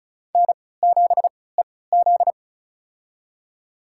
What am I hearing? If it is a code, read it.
Morse code, heard as a keyed tone, about 34 wpm, spaced as for 20 wpm, 700 Hz. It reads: N7EZ
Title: USA Call Signs - 4 Characters - 3x Character Spacing - 35wpm